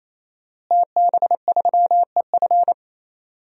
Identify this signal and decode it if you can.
Morse code, heard as a keyed tone, about 28 wpm, 700 Hz. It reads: TB3EF
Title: DXCC-100 Call Signs - 5 Characters - Repeated 3x - 28wpm